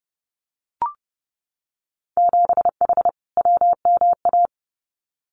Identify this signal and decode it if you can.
Morse code, heard as a keyed tone, about 30 wpm, 700 Hz. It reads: 7H WMA
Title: Winter Field Day Exchanges Encore 30wpm